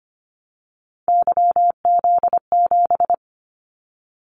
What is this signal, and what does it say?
Morse code, heard as a keyed tone, about 25 wpm, 700 Hz. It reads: YZ7
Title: Callsign Prefixes for Top 25 DXCC - 25wpm